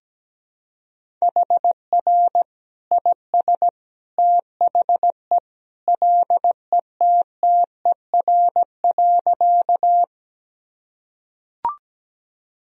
Morse code, heard as a keyed tone, about 17 wpm, 700 Hz. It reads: HR IS THE LETTER.
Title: Sentences from Top 300 Words and Abbreviations 17wpm